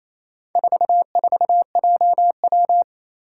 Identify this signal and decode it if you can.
Morse code, heard as a keyed tone, about 27 wpm, 700 Hz. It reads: 44JW
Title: Four Challenging Letter-Number Combinations 28wpm